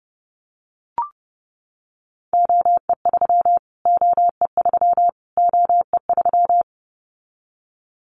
Morse code, heard as a keyed tone, about 30 wpm, 700 Hz. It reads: OE3 OE3 OE3
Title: Callsign Prefixes for Top 25 DXCC - Repeated 3x - 30wpm